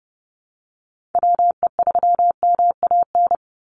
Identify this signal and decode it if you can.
Morse code, heard as a keyed tone, about 30 wpm, 700 Hz. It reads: WE3MAN
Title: USA Call Signs 30wpm